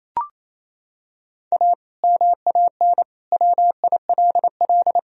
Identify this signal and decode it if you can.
Morse code, heard as a keyed tone, about 28 wpm, 700 Hz. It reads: A MAN WILL
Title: Sentences from Top 1000 Words - Review Part 3 - 28wpm